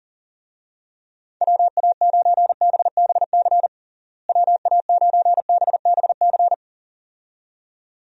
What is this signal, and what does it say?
Morse code, heard as a keyed tone, about 40 wpm, 700 Hz. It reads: WA9BBC WA9BBC
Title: USA Call Signs - 6 Characters - Repeated 2x - 40wpm